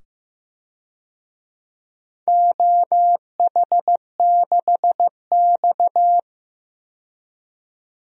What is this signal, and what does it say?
Morse code, heard as a keyed tone, about 16 wpm, 700 Hz. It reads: OH6X
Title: DXCC-100 Call Signs - 4 Characters - 15wpm